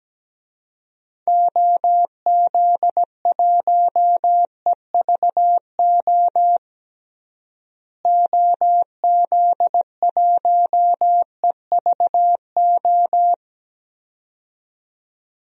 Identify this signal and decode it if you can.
Morse code, heard as a keyed tone, about 17 wpm, 700 Hz. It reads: OZ1EVO OZ1EVO
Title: DXCC-100 Call Signs - 6 Characters - Repeated 2x - 17wpm